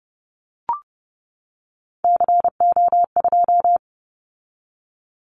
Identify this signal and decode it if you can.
Morse code, heard as a keyed tone, about 30 wpm, 700 Hz. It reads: CO2
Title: Callsign Prefixes for Top 100 DXCC - 30wpm